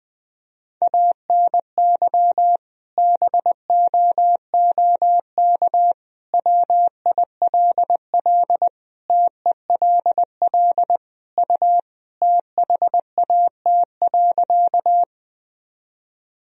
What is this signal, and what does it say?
Morse code, heard as a keyed tone, about 20 wpm, 700 Hz. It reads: ANY BOOK WILL TELL U THAT.
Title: Sentences from Top 300 Words and Abbreviations 20wpm